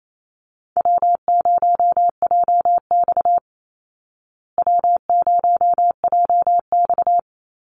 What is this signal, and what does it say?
Morse code, heard as a keyed tone, about 28 wpm, 700 Hz. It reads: W0JX W0JX
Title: Call Signs Repeated 3x - 28wpm